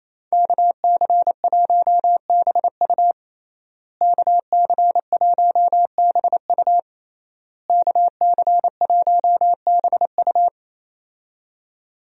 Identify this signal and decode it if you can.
Morse code, heard as a keyed tone, about 27 wpm, 700 Hz. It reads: KC1BU KC1BU KC1BU
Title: USA Call Signs - 5 Characters - Repeated 3x - 28wpm